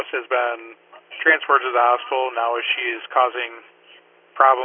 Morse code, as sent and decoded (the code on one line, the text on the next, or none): none